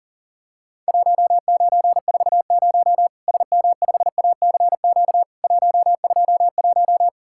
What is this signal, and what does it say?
Morse code, heard as a keyed tone, about 40 wpm, 700 Hz. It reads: 1940 SM5ACQ 121
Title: Worked All Europe DX Contest - QTC Exchange - 40wpm